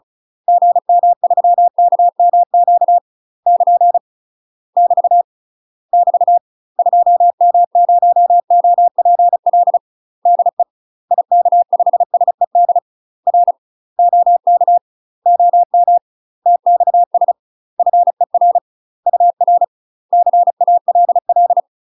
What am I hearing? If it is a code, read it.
Morse code, heard as a keyed tone, about 34 wpm, 700 Hz. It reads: GM3KMQ <KN> <BT> <BT> 2M0OPL DE IK5SED R OK OM TXS FER UR CALL
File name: UK MASTER QSOs TRAINING / 35WPM / UK MASTER QSOs 0008